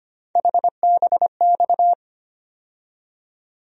Morse code, heard as a keyed tone, about 25 wpm, 700 Hz. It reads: HBX
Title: USA Call Sign Postfixes - 3 Letters 25wpm